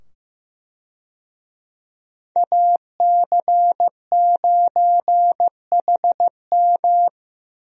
Morse code, heard as a keyed tone, about 15 wpm, 700 Hz. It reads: AC9HM
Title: USA Call Signs - 5 Characters - 15wpm